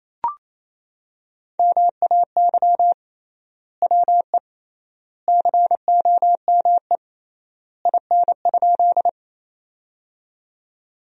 Morse code, heard as a keyed tone, about 28 wpm, 700 Hz. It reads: MAY WE COME IN?
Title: 3x Word-Spacing - Sentences from Top 100 Words - 28wpm